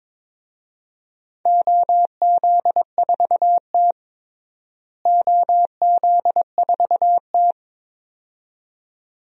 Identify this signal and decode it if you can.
Morse code, heard as a keyed tone, about 22 wpm, 700 Hz. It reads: OZ4T OZ4T
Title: DXCC-100 Call Signs - 4 Characters - Repeated 2x - 22wpm